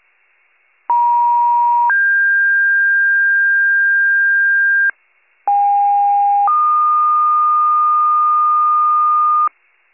Two-tone page, heard as a one-second tone, then a three-second tone, about 950 then 1600 Hz, about 810 then 1220 Hz, with noise. It is heard from 0.9 to 9.5 s.